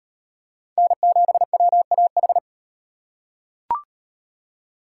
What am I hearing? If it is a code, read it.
Morse code, heard as a keyed tone, about 38 wpm, 700 Hz. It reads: N7WAH